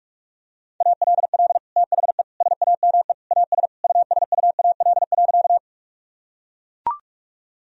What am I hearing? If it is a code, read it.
Morse code, heard as a keyed tone, about 44 wpm, 700 Hz. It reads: ALL THE SAME AS USUAL.
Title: Sentences from Top 1000 Words - Review Part 2 - 45wpm